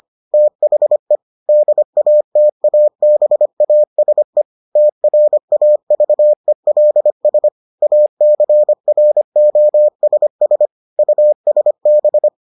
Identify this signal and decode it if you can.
Morse code, heard as a keyed tone, about 25 wpm, 600 Hz. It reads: THE DATABASE TRAVELS ACROSS USB